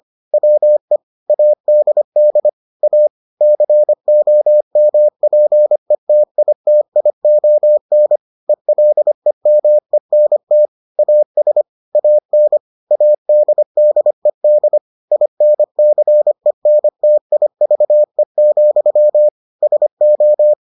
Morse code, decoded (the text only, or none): WE ADD A COMPETITION ELEMENT AS AN ADDED INCENTIVE, SO